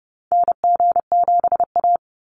Morse code, heard as a keyed tone, about 28 wpm, 700 Hz. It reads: NG7A
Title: NA SOTA Activator Callsigns 30wpm